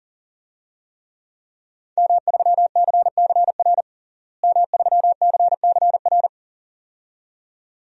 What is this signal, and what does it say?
Morse code, heard as a keyed tone, about 41 wpm, 700 Hz. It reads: M3CCR M3CCR
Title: DXCC-100 Call Signs - 5 Characters - Repeated 2x - 40wpm